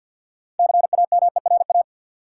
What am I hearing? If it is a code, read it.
Morse code, heard as a keyed tone, about 53 wpm, 700 Hz. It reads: CAMERA
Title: Six Letter Words - 50wpm